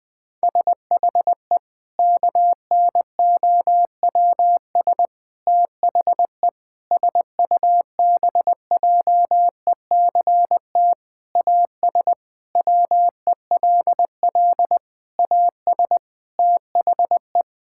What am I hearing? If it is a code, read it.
Morse code, heard as a keyed tone, about 20 wpm, 700 Hz. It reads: SHE KNOWS THE SUBJECT AS WELL AS THE
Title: Sentences with 4-Word Phrases - 20wpm